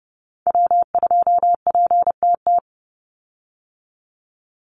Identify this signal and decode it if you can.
Morse code, heard as a keyed tone, about 30 wpm, 700 Hz. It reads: W2PTT
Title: Call Signs - 30wpm